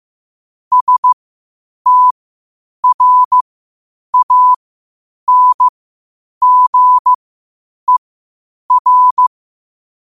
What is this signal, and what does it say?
Morse code, heard as a keyed tone, about 15 wpm, spaced as for 8 wpm, 1000 Hz. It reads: STRANGER
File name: qotdfiles / 8wpm